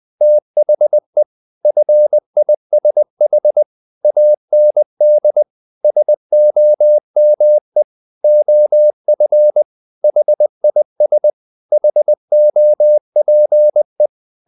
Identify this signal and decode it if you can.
Morse code, heard as a keyed tone, about 20 wpm, 600 Hz. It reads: THE FISH AND SOME OF HIS HOPE
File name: the old man and the sea part 2 20wpm